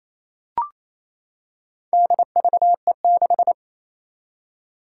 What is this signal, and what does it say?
Morse code, heard as a keyed tone, about 28 wpm, 700 Hz. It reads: DVE6
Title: Four Challenging Letter-Number Combinations 28wpm